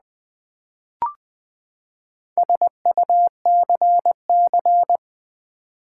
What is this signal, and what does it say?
Morse code, heard as a keyed tone, about 20 wpm, 700 Hz. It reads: SUCC
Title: Build A Word - 4 Letters At A Time - 10-Letter Words - 20wpm